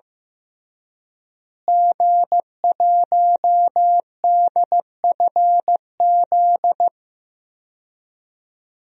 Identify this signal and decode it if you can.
Morse code, heard as a keyed tone, about 15 wpm, 700 Hz. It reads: G1DFZ
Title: DXCC-100 Call Signs - 5 Characters - Repeated 3x - 15wpm